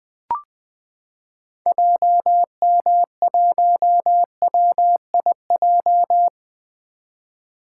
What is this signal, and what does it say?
Morse code, heard as a keyed tone, about 20 wpm, 700 Hz. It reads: JM1WIJ
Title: 2023 POTA Activator Call Signs - 20wpm